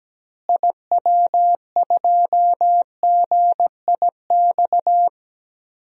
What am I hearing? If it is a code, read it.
Morse code, heard as a keyed tone, about 17 wpm, 700 Hz. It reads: IW2GIX
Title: DXCC-100 Call Signs - 6 Characters - Repeated 2x - 17wpm